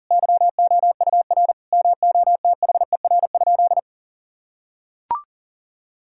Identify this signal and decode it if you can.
Morse code, heard as a keyed tone, about 40 wpm, 700 Hz. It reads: YOUR MOTHER?
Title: Sentences from Top 1000 Words - Review Part 2 - 40wpm